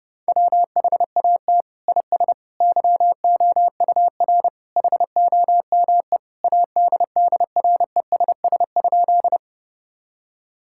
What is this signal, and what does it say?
Morse code, heard as a keyed tone, about 30 wpm, 700 Hz. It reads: WHAT IS YOUR HOME ADDRESS?
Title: Sentences from Top 1000 Words - Review Part 6 - 30wpm